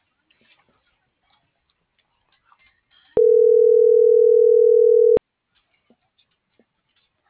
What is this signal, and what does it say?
Ringback tone (North America), 440 + 480 Hz.